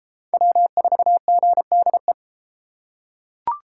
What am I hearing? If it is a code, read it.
Morse code, heard as a keyed tone, about 32 wpm, 700 Hz. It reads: W4GDE